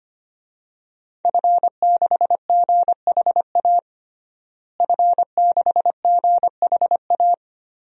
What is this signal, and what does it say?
Morse code, heard as a keyed tone, about 25 wpm, 700 Hz. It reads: F6GHA F6GHA
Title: DXCC-100 Call Signs - 5 Characters - Repeated 2x - 25wpm